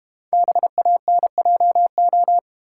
Morse code, heard as a keyed tone, about 32 wpm, 700 Hz. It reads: BANJO